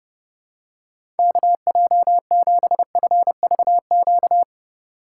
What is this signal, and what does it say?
Morse code, heard as a keyed tone, about 31 wpm, 690 Hz. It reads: KJ7FVQ
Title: USA Call Signs - 4x Character Spacing - 30wpm